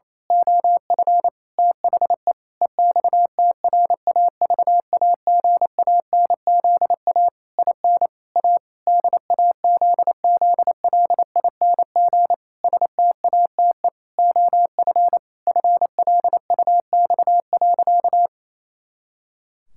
Morse code, heard as a keyed tone, about 28 wpm, 700 Hz. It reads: OF THE EXTRAVAGANZA IN A DAZZLING STATE OF FLUX.
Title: Pangram Challenge v2 - 28wpm